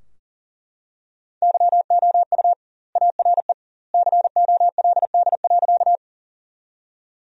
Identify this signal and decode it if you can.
Morse code, heard as a keyed tone, about 40 wpm, 700 Hz. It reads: YOU ARE COLD.
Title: 2x Word-Spacing - Sentences from Top 400 Words - 40wpm